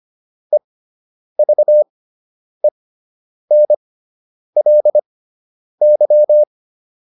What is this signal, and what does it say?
Morse code, heard as a keyed tone, about 25 wpm, spaced as for 9 wpm, 600 Hz. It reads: EVENLY